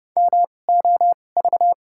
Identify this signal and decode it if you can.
Morse code, heard as a keyed tone, about 29 wpm, spaced as for 21 wpm, 700 Hz. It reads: MOV
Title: 2x Character Spacing - Top 200 Words - 30wpm